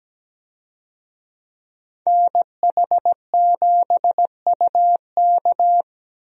Morse code, heard as a keyed tone, about 17 wpm, 700 Hz. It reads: NH7UK
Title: USA Call Signs - 5 Characters - Repeated 2x - 17wpm